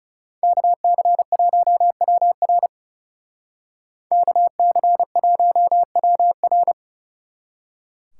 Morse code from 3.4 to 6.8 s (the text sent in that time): KC1WR